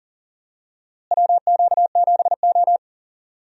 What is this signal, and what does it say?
Morse code, heard as a keyed tone, about 40 wpm, 700 Hz. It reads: WQ7O
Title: USA Call Signs - 4 Characters - Repeated 2x - 40wpm